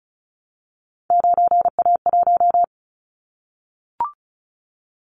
Morse code, heard as a keyed tone, about 34 wpm, 700 Hz. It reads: 9A1
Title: Callsign Prefixes for Top 50 DXCC - 35wpm